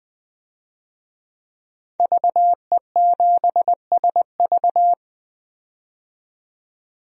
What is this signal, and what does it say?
Morse code, heard as a keyed tone, about 21 wpm, 700 Hz. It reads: VE7SV
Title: DXCC-100 Call Signs - 5 Characters - 20wpm